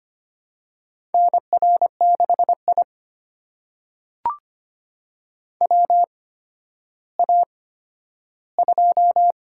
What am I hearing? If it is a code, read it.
Morse code, heard as a keyed tone, about 25 wpm, spaced as for 12 wpm, 700 Hz. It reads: NR6I WA2
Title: USA Call Signs - 4 Characters - 8x Character Spacing - 25wpm